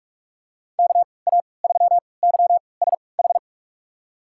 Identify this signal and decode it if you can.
Morse code, heard as a keyed tone, about 46 wpm, spaced as for 26 wpm, 700 Hz. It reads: KA3YSH
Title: USA Call Signs - 6 Characters - 3x Character Spacing - 45wpm